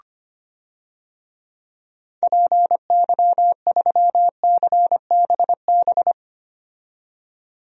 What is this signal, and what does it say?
Morse code, heard as a keyed tone, about 25 wpm, 700 Hz. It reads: PY3CBB
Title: DXCC-100 Call Signs - 6 Characters - 25wpm